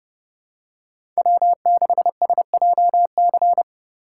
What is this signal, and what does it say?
Morse code, heard as a keyed tone, about 29 wpm, 700 Hz. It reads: W6SJC